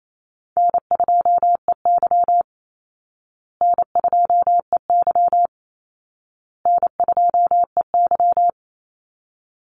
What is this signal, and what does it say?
Morse code, heard as a keyed tone, about 28 wpm, 700 Hz. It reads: N2EY N2EY N2EY
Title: Call Signs Repeated 2x - 28wpm